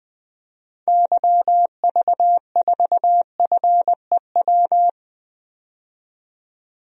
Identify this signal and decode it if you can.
Morse code, heard as a keyed tone, about 20 wpm, 700 Hz. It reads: YV4FEW
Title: DXCC-100 Call Signs - 6 Characters - 20wpm